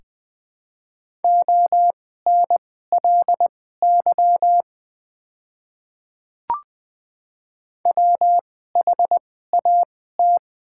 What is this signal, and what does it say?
Morse code, heard as a keyed tone, about 20 wpm, spaced as for 14 wpm, 700 Hz. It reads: ONLY WHAT
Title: 2x Character Spacing - Top 300 Words - 20wpm